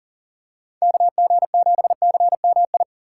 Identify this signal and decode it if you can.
Morse code, heard as a keyed tone, about 40 wpm, 700 Hz. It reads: KG7CMI
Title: USA Call Signs - 6 Characters - 6x Character Spacing - 40wpm